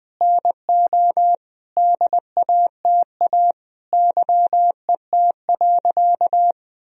Morse code, heard as a keyed tone, about 21 wpm, 700 Hz. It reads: NO DATA YET.
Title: Sentences from Top 1000 Words - Review Part 3 - 20wpm